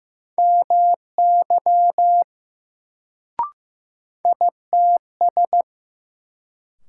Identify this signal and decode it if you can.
Morse code, heard as a keyed tone, about 15 wpm, 700 Hz. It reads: MY ITS
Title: Sets of 2 Words - New Word Drill - 15wpm